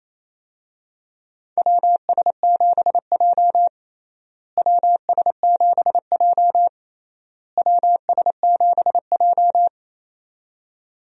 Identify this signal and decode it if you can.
Morse code, heard as a keyed tone, about 28 wpm, 700 Hz. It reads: WS7J WS7J WS7J
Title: USA Call Signs - 4 Characters - Repeated 3x - 28wpm